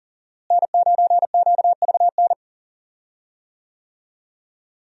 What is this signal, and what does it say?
Morse code, heard as a keyed tone, about 37 wpm, 700 Hz. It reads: N9QVN
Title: USA Call Signs - 5 Characters - Repeated 2x - 40wpm